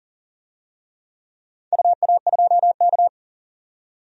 Morse code, heard as a keyed tone, about 40 wpm, 700 Hz. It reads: UA2K
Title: DXCC-100 Call Signs - 4 Characters - 40wpm